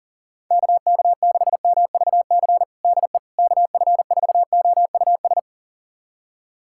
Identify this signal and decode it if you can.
Morse code, heard as a keyed tone, about 40 wpm, 700 Hz. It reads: KK6MVC DE KF4OUS
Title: USA Callsign DE Callsign - 6 Characters - 40wpm